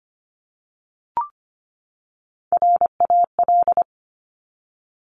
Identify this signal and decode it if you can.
Morse code, heard as a keyed tone, about 25 wpm, 700 Hz. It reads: RAL
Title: Top Three Letter Combinations in Words 25wpm